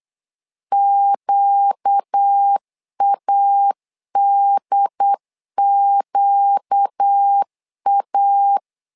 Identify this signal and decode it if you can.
Morse code, heard as a keyed tone, about 8 wpm, 800 Hz. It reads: QADQA
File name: 4 Q 8wpm